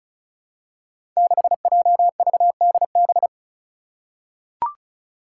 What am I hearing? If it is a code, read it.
Morse code, heard as a keyed tone, about 36 wpm, 700 Hz. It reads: 6JVDB
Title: Five Challenging Letter-Number Combinations 35wpm